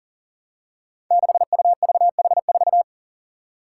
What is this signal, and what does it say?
Morse code, heard as a keyed tone, about 40 wpm, 700 Hz. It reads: 6UVH4
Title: Five Challenging Letter-Number Combinations 40wpm